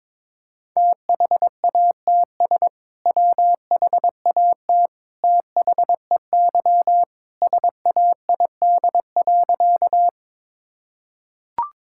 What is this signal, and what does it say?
Morse code, heard as a keyed tone, about 22 wpm, 700 Hz. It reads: THATS WHAT THEY SAID.
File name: practice / 22wpm / Sentences with 2-Word Phrases - 22wpm